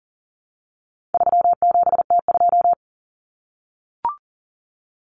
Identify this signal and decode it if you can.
Morse code, heard as a keyed tone, about 40 wpm, 700 Hz. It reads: 37T2
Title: Common Cut Numbers - 4-Digit 40wpm